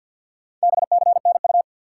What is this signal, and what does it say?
Morse code, heard as a keyed tone, about 53 wpm, 700 Hz. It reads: BCNU
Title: Top 90 QSO Elements 50wpm